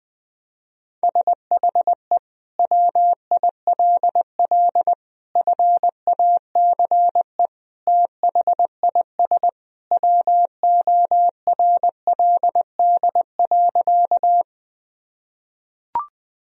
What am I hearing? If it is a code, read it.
Morse code, heard as a keyed tone, about 20 wpm, 700 Hz. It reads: SHE WILL FACE THIS WORLD.
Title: Sentences from Top 1000 Words - Review Part 2 - 20wpm